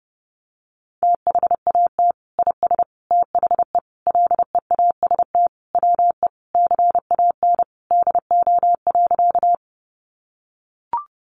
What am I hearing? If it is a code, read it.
Morse code, heard as a keyed tone, about 30 wpm, 700 Hz. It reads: THAT IS THE LEAST WE CAN DO.
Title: Sentences from Top 700 Words 30wpm